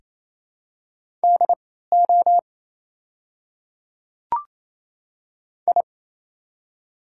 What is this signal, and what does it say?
Morse code, heard as a keyed tone, about 28 wpm, spaced as for 16 wpm, 690 Hz. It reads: DO I
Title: 3x Character Spacing - Top 100 Words - 28wpm